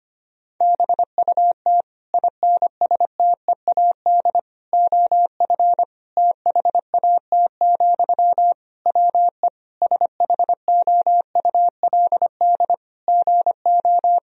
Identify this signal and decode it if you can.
Morse code, heard as a keyed tone, about 25 wpm, 700 Hz. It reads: BUT INSTEAD OF THAT, WE SHOULD GO
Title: Sentences from Top 1000 Words - Review Part 4 - 25wpm